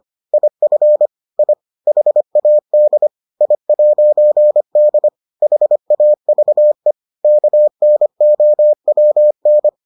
Morse code, heard as a keyed tone, about 25 wpm, 600 Hz. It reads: IF I HAD I'D HAVE KNOWN